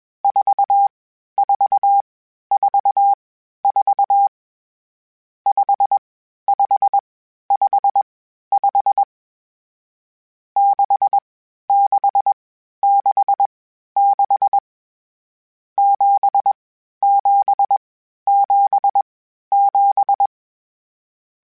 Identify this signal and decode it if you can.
Morse code, heard as a keyed tone, about 21 wpm, spaced as for 12 wpm, 800 Hz. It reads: 4444 5555 6666 7777
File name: Misc / Misc-mp3s / Numb @20